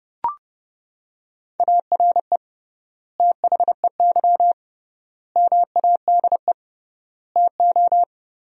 Morse code, heard as a keyed tone, about 30 wpm, 700 Hz. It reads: ARE THEY MADE TO
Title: 3x Word-Spacing - Sentences from Top 400 Words - 30wpm